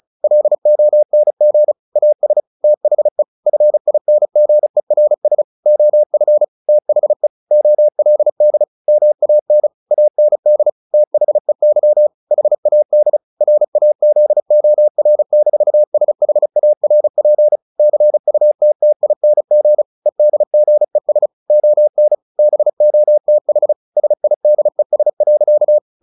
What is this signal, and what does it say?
Morse code, heard as a keyed tone, about 34 wpm, 600 Hz. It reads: LONG AS THE FINGERS OF THE OLD MAN AND THEY HAD RAZOR-SHARP CUTTING EDGES ON BOTH SIDES.
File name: the old man and the sea 2 35wpm